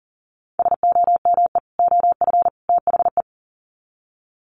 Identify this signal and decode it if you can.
Morse code, heard as a keyed tone, about 40 wpm, 700 Hz. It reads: SOME OF THE
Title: Sets of 3 Words 40wpm